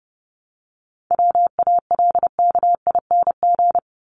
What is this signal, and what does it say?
Morse code, heard as a keyed tone, about 30 wpm, 700 Hz. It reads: WALKING
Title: ING Suffix - Top 100 Words with Base Words 30wpm